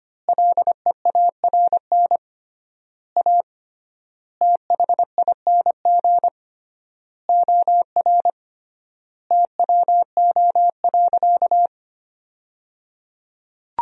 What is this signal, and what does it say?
Morse code, heard as a keyed tone, about 25 wpm, 700 Hz. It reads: LEARN A THING OR TWO.
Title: 3x Word-Spacing - Sentences from Top 300 Words - 25wpm